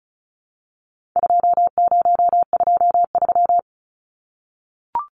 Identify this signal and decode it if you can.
Morse code, heard as a keyed tone, about 34 wpm, 700 Hz. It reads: 2023
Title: Numbers - 4-Digit 35wpm